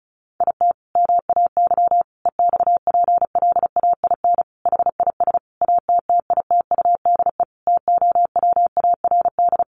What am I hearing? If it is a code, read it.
Morse code, heard as a keyed tone, about 34 wpm, 700 Hz. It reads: IT MAY EXPLAIN HIS ATTITUDE TOWARD
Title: Sentences from Top 1300 Words 35wpm